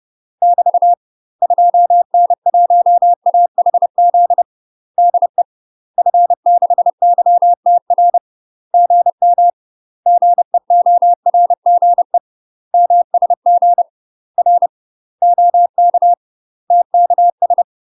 Morse code, heard as a keyed tone, about 30 wpm, 700 Hz. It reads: <BT> 2N1AHZ DE F6YTR GM GEORGE MSG R OK TKS
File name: UK MASTER QSOs TRAINING / 30WPM / UK MASTER QSOs 0002